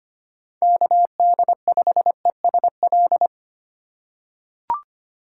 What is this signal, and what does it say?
Morse code, heard as a keyed tone, about 25 wpm, 700 Hz. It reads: KD5ESL
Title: USA Call Signs - 6 Characters - 4x Character Spacing - 25wpm